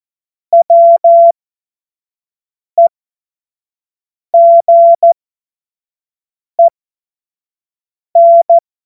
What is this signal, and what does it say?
Morse code, heard as a keyed tone, about 14 wpm, spaced as for 5 wpm, 680 Hz. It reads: WEGEN
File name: Morsekurs / Witze / Witz 17. Mai 2022 25 F70